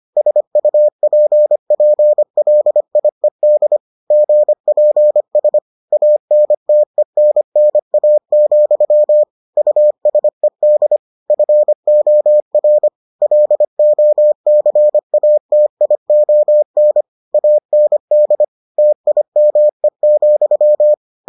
Morse code, heard as a keyed tone, about 25 wpm, 600 Hz. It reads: SUPPLIED GPS ANTENNA, USED FOR LOCATION AND TIME,